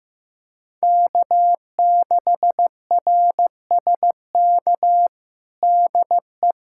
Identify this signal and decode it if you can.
Morse code, heard as a keyed tone, about 15 wpm, 700 Hz. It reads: K6RSK DE
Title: USA Callsign DE Callsign - 5 Characters - 15wpm